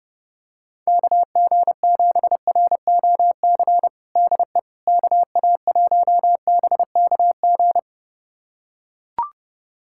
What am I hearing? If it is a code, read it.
Morse code, heard as a keyed tone, about 30 wpm, 700 Hz. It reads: KG7ROC DE KA1BKG
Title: USA Callsign DE Callsign - 6 Characters - 30wpm